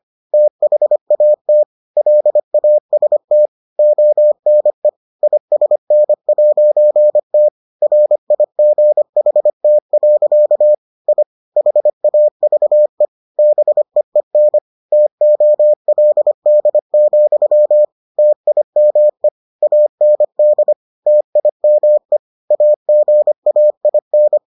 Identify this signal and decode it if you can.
Morse code, heard as a keyed tone, about 25 wpm, 600 Hz. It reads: THAT LAST ONE ISN'T RIGHT. I HAVE BEEN TOLD, TIME AND TIME AGAIN